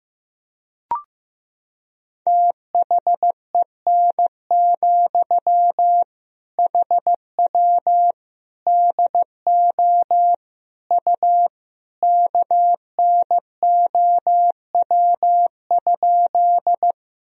Morse code, heard as a keyed tone, about 15 wpm, 700 Hz. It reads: THEN, HW DO U KNOW?
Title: Sentences from Top 100 Words and Abbreviations 15wpm